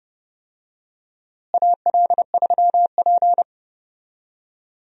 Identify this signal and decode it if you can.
Morse code, heard as a keyed tone, about 30 wpm, 700 Hz. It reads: AL3P
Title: USA Call Signs - 4 Characters - Repeated 3x - 30wpm